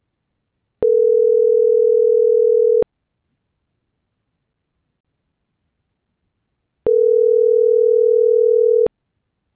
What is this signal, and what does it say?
Ringback tone (North America), 440 + 480 Hz.